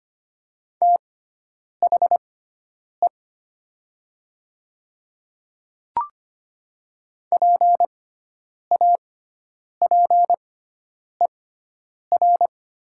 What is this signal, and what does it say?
Morse code, heard as a keyed tone, about 25 wpm, spaced as for 9 wpm, 700 Hz. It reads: THE PAPER
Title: 6x Character Spacing - Top 300 Words - 25wpm